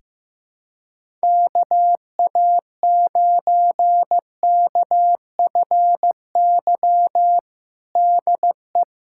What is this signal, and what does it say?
Morse code, heard as a keyed tone, about 15 wpm, 700 Hz. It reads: KA9KFY DE
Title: USA Callsign DE Callsign - 6 Characters - 15wpm